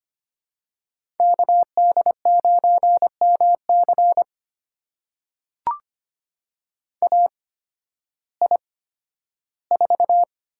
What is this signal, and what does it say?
Morse code, heard as a keyed tone, about 25 wpm, spaced as for 13 wpm, 700 Hz. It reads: KD9MC AI4
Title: USA Call Signs - 5 Characters - 8x Character Spacing - 25wpm